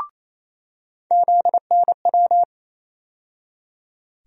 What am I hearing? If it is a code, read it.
Morse code, heard as a keyed tone, about 27 wpm, 700 Hz. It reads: ZNW
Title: USA Call Sign Postfixes - 3 Letters 28wpm